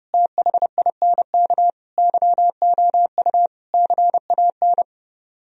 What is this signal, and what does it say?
Morse code, heard as a keyed tone, about 29 wpm, 700 Hz. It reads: THINK YOU CAN